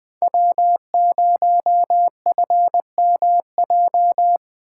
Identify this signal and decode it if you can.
Morse code, heard as a keyed tone, about 20 wpm, 700 Hz. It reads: W0FMJ
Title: Speed Racing - USA 1x3 Call Signs 17wpm